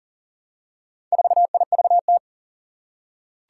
Morse code, heard as a keyed tone, about 38 wpm, 700 Hz. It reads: 4IVT
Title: Four Challenging Letter-Number Combinations 40wpm